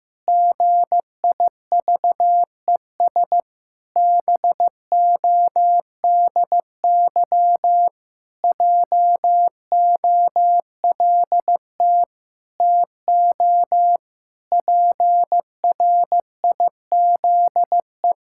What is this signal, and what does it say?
Morse code, heard as a keyed tone, about 15 wpm, 700 Hz. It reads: GIVES BODY JOLT TO PRIZE
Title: Pangram Challenge v2 - 15wpm